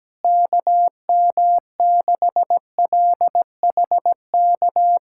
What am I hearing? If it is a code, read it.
Morse code, heard as a keyed tone, about 17 wpm, 700 Hz. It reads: KM6LHK
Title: USA Callsign DE Callsign - 6 Characters - 17wpm